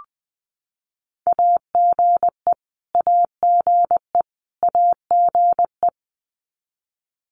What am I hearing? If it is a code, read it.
Morse code, heard as a keyed tone, about 20 wpm, 700 Hz. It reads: AGE AGE AGE
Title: Top 500 Words Repeated 20wpm